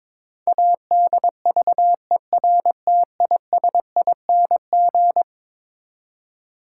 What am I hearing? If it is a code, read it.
Morse code, heard as a keyed tone, about 22 wpm, 700 Hz. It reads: ADVERTISING
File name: practice / 22wpm / Eleven Letter Words - 22wpm